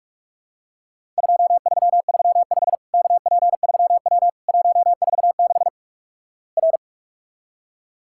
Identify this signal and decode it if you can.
Morse code, heard as a keyed tone, about 46 wpm, 700 Hz. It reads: 2335 KP3W 146 R
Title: Worked All Europe DX Contest - QTC Exchange - 45wpm